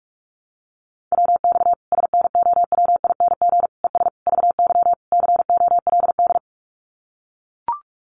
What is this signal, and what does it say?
Morse code, heard as a keyed tone, about 44 wpm, 700 Hz. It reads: WX SNOWING ES VY COLD
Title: Common QSO Phrases 45wpm